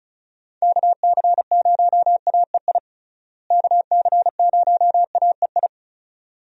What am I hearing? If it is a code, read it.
Morse code, heard as a keyed tone, about 34 wpm, 700 Hz. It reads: KC0AEI KC0AEI
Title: USA Call Signs - 6 Characters - Repeated 2x - 35wpm